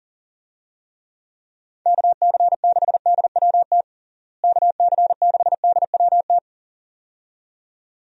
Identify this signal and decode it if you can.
Morse code, heard as a keyed tone, about 40 wpm, 700 Hz. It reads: KC6DWT KC6DWT
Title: USA Call Signs - 6 Characters - Repeated 2x - 40wpm